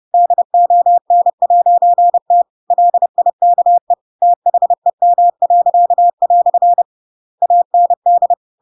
Morse code, heard as a keyed tone, about 30 wpm, 700 Hz. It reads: DON'T LIKE THEM." AND
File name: Pinocchio / Pinocchio ch7 30 30 ik1hge